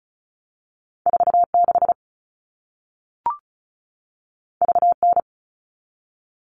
Morse code, heard as a keyed tone, about 36 wpm, 700 Hz. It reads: 46 VN
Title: All Cut Numbers - 2-Digit 35wpm